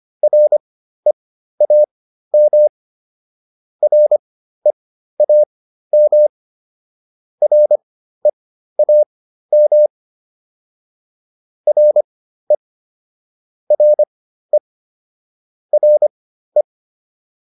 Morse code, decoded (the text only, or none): REAM REAM REAM RE RE RE